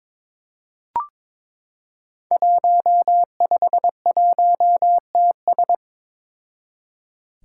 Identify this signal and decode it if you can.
Morse code, heard as a keyed tone, about 22 wpm, 700 Hz. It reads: 151TS
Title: Five Challenging Letter-Number Combinations 22wpm